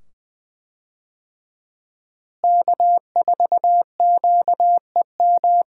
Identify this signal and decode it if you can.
Morse code, heard as a keyed tone, about 20 wpm, 700 Hz. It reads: K4QEM
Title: USA Call Signs - 5 Characters - Repeated 2x - 20wpm